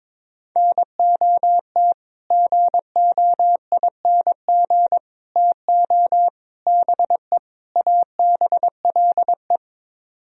Morse code, heard as a keyed tone, about 22 wpm, 700 Hz. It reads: NOT GOING TO BE ABLE
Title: Speed-Racing - Sets of 5 Words 22wpm